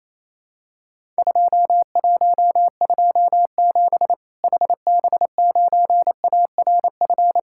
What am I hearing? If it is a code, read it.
Morse code, heard as a keyed tone, about 28 wpm, 700 Hz. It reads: 2127 HB9ARF
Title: Worked All Europe DX Contest - QTC Exchange - 28wpm